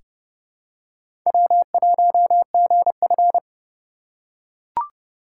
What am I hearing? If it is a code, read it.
Morse code, heard as a keyed tone, about 30 wpm, 700 Hz. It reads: W1GF
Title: USA Call Signs - 4 Characters - 3x Character Spacing - 30wpm